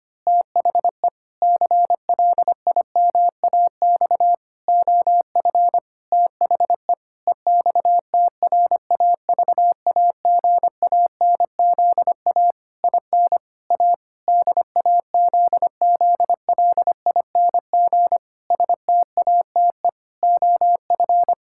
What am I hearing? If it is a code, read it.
Morse code, heard as a keyed tone, about 25 wpm, 700 Hz. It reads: THE CLIMAX OF THE EXTRAVAGANZA IN A DAZZLING STATE OF
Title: Pangram Challenge v2 - 25wpm